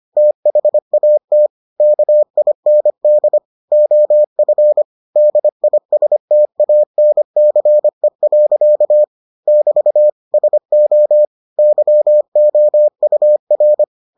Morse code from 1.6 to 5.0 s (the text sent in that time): KIND OF